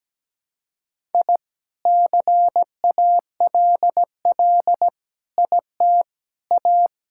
Morse code, heard as a keyed tone, about 17 wpm, 700 Hz. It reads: I CALL IT A
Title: Sentences from Top 1000 Words - Review Part 6 - 17wpm